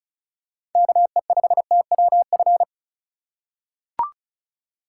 Morse code, heard as a keyed tone, about 34 wpm, 700 Hz. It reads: KE5TWF
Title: USA Call Signs - 6 Characters - 2x Character Spacing - 35wpm